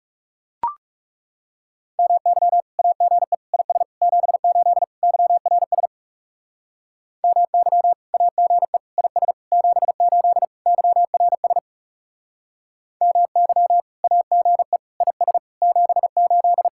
Morse code, heard as a keyed tone, about 40 wpm, 700 Hz. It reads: MY AGE IS 78 YRS MY AGE IS 78 YRS MY AGE IS 78